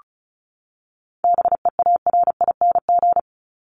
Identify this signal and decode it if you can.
Morse code, heard as a keyed tone, about 34 wpm, 700 Hz. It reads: BEARING
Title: ING Suffix - Top 100 Words with Base Words 35wpm